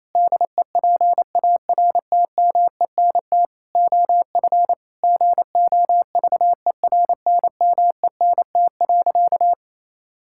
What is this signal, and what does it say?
Morse code, heard as a keyed tone, about 28 wpm, 700 Hz. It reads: DEPARTMENT OF GOVERNMENT.
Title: Sentences from Top 1000 Words - Review Part 5 - 28wpm